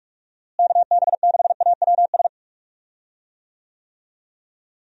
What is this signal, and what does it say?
Morse code, heard as a keyed tone, about 41 wpm, 700 Hz. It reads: KB6AWS
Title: USA Call Signs - 6 Characters - Repeated 3x - 45wpm